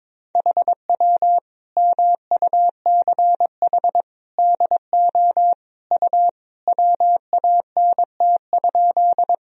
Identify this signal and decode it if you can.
Morse code, heard as a keyed tone, about 22 wpm, 700 Hz. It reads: HW MUCH DO U WANT?
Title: Sentences from Top 200 Words and Abbreviations 22wpm